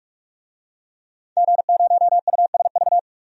Morse code, heard as a keyed tone, about 43 wpm, 700 Hz. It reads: G0USV
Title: DXCC-100 Call Signs - 5 Characters - 45wpm